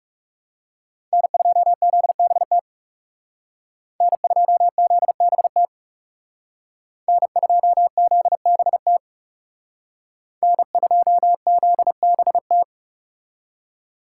Morse code, heard as a keyed tone, about 35 wpm, 700 Hz. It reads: N2ZBT N2ZBT N2ZBT N2ZBT
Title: Speed Racing - USA 1x3 Call Signs 30wpm